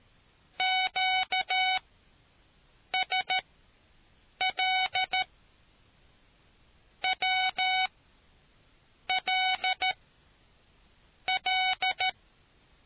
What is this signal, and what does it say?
Morse code, heard as a keyed tone, about 13 wpm, spaced as for 6 wpm, 750 Hz. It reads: QSL WL L